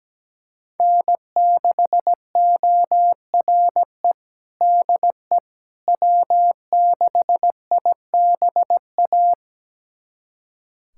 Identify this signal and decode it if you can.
Morse code, heard as a keyed tone, about 17 wpm, 700 Hz. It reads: N6ORE DE W6IBA
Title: USA Callsign DE Callsign - 5 Characters - 17wpm